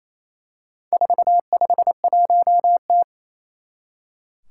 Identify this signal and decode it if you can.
Morse code, heard as a keyed tone, about 29 wpm, 700 Hz. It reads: 451T
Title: Four Challenging Letter-Number Combinations 28wpm